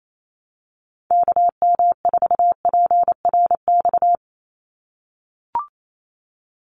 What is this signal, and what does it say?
Morse code, heard as a keyed tone, about 28 wpm, 700 Hz. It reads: KM4PRX
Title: USA Call Signs 28wpm